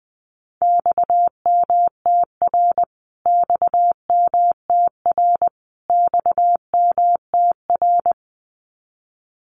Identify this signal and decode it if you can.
Morse code, heard as a keyed tone, about 20 wpm, 700 Hz. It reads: XMTR XMTR XMTR
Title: Abbreviations Repeated 3x - 20wpm